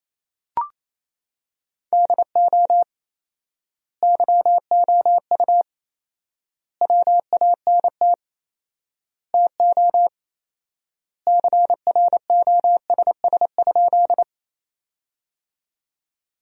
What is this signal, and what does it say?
Morse code, heard as a keyed tone, about 28 wpm, 700 Hz. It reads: DO YOU WANT TO CROSS?
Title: 4x Word-Spacing - Sentences from Top 300 Words - 28wpm